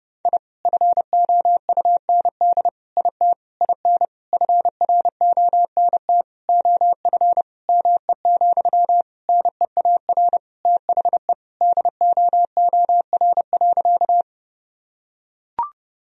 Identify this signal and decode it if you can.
Morse code, heard as a keyed tone, about 32 wpm, 700 Hz. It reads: I FOUND IT IN FRONT OF ME, NEAR THE DOOR.
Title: Sentences from Top 1000 Words - Review Part 2 - 30wpm